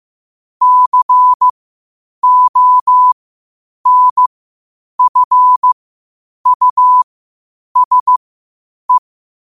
Morse code, heard as a keyed tone, about 15 wpm, spaced as for 8 wpm, 1000 Hz. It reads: CONFUSE
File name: qotdfiles / 8wpm